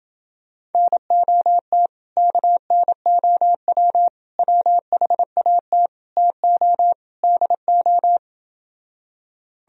Morse code, heard as a keyed tone, about 29 wpm, 700 Hz. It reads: NOT KNOW WHAT TO DO